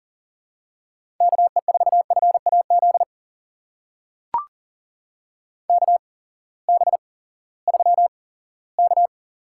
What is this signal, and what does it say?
Morse code, heard as a keyed tone, about 40 wpm, spaced as for 20 wpm, 700 Hz. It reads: KE4FAZ KB3K
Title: USA Call Signs - 6 Characters - 8x Character Spacing - 40wpm